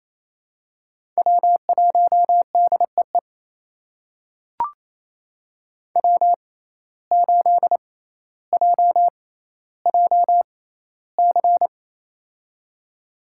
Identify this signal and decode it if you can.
Morse code, heard as a keyed tone, about 28 wpm, spaced as for 14 wpm, 700 Hz. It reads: W1DEE W8JJC
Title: USA Call Signs - 5 Characters - 6x Character Spacing - 28wpm